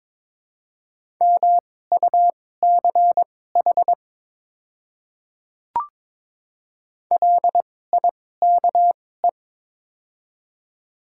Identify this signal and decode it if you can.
Morse code, heard as a keyed tone, about 22 wpm, spaced as for 16 wpm, 700 Hz. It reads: MUCH LIKE
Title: 2x Character Spacing - Top 500 Words - 22wpm